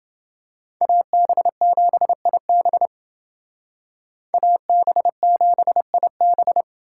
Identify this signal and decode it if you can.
Morse code, heard as a keyed tone, about 28 wpm, 700 Hz. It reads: AB7IB AB7IB